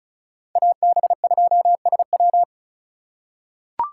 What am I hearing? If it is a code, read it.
Morse code, heard as a keyed tone, about 36 wpm, 700 Hz. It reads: AB2SW
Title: USA Call Signs - 5 Characters - 2x Character Spacing - 35wpm